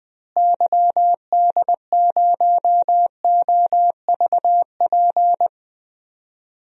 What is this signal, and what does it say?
Morse code, heard as a keyed tone, about 20 wpm, 700 Hz. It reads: YD0OVP
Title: DXCC-100 Call Signs - 6 Characters - Repeated 3x - 20wpm